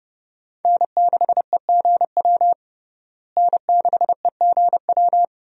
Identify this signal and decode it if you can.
Morse code, heard as a keyed tone, about 30 wpm, 700 Hz. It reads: N6EGW N6EGW
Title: USA Call Signs - 5 Characters - Repeated 3x - 30wpm